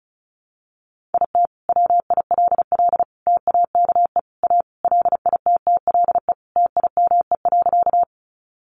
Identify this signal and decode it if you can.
Morse code, heard as a keyed tone, about 34 wpm, 700 Hz. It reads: IT WILL TAKE A LITTLE TIME.
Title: Sentences from Top 200 Words 35wpm